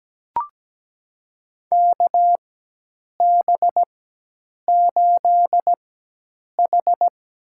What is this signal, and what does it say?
Morse code, heard as a keyed tone, about 17 wpm, spaced as for 8 wpm, 700 Hz. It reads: KB8H
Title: USA Call Signs - 4x Character Spacing - 17wpm